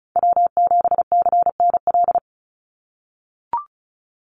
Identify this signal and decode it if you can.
Morse code, heard as a keyed tone, about 34 wpm, 700 Hz. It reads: W7CNL
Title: NA SOTA Chaser Callsigns 35wpm